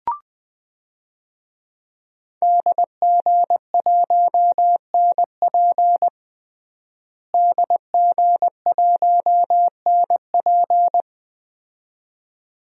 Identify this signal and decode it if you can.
Morse code, heard as a keyed tone, about 20 wpm, 700 Hz. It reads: DG1NP DG1NP
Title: DXCC-100 Call Signs - 5 Characters - Repeated 2x - 20wpm